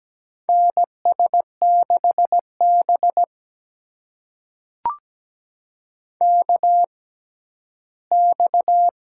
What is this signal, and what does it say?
Morse code, heard as a keyed tone, about 17 wpm, spaced as for 12 wpm, 700 Hz. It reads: NS6B KX
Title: USA Call Signs - 4 Characters - 6x Character Spacing - 17wpm